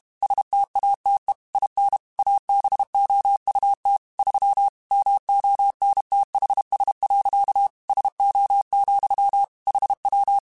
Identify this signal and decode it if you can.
Morse code, heard as a keyed tone, about 32 wpm, 800 Hz. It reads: STATE IN ABOUT 3 MONTHS. SO, HW
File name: QSOs / QSOs-mp3s / WD0FBT 1@30